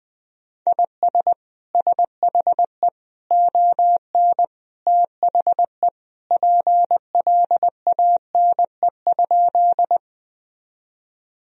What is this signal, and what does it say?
Morse code, heard as a keyed tone, about 20 wpm, 700 Hz. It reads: IS SHE ON THE PLANE?
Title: Sentences from Top 1000 Words - Review Part 3 - 20wpm